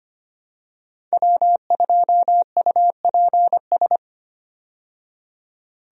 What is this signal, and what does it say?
Morse code, heard as a keyed tone, about 25 wpm, 700 Hz. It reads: W2UPS
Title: USA Call Signs - 5 Characters - 25wpm